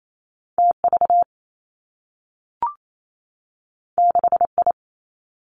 Morse code, heard as a keyed tone, about 28 wpm, 700 Hz. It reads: T V 6 I
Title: Two Challenging Letter-Number Combinations 28wpm